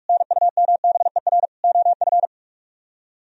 Morse code, heard as a keyed tone, about 48 wpm, 700 Hz. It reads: NUMBER OF